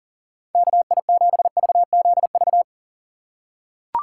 Morse code, heard as a keyed tone, about 40 wpm, 700 Hz. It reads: KI7VZV
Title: USA Call Signs - 6 Characters - 3x Character Spacing - 40wpm